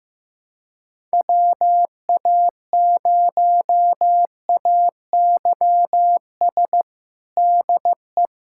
Morse code, heard as a keyed tone, about 15 wpm, 700 Hz. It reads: WA0AYS DE
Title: USA Callsign DE Callsign - 6 Characters - 15wpm